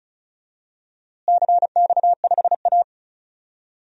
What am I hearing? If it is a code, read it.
Morse code, heard as a keyed tone, about 36 wpm, 700 Hz. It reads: CX5A
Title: DXCC-100 Call Signs - 4 Characters - Repeated 3x - 35wpm